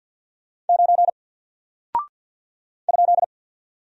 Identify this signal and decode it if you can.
Morse code, heard as a keyed tone, about 48 wpm, 700 Hz. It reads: 9 ?
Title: Single Character (Letter:Number:Punctuation:Prosign) Alt 50wpm